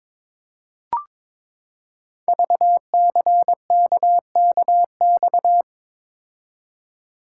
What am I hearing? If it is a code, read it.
Morse code, heard as a keyed tone, about 22 wpm, 700 Hz. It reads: VCKKX
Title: Code Groups - 5 Letters - 22wpm